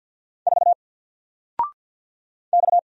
Morse code, heard as a keyed tone, about 50 wpm, 700 Hz. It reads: 4 X
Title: Single Letter-Number 50wpm